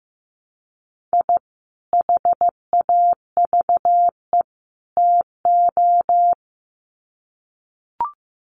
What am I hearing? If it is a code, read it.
Morse code, heard as a keyed tone, about 15 wpm, 700 Hz. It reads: I HAVE TO
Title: Sets of 3 Words 15wpm